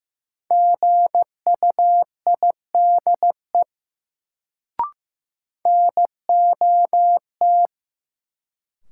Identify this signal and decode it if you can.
Morse code, heard as a keyed tone, about 16 wpm, 700 Hz. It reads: GUIDE NOT
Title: Sentences from Top 600 Words - New Word Drill - 15wpm